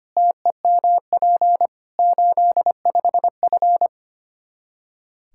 Morse code, heard as a keyed tone, about 25 wpm, 700 Hz. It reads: TEMP 85F
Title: Practice QSO - Exchange 2 - Info - 25wpm